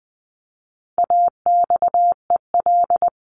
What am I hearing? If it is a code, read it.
Morse code, heard as a keyed tone, about 20 wpm, 700 Hz. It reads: AXEL
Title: CWT Exchange 20wpm